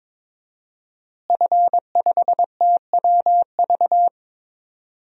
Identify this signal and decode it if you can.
Morse code, heard as a keyed tone, about 23 wpm, 700 Hz. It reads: F5TWV
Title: DXCC-100 Call Signs - 5 Characters - Repeated 3x - 22wpm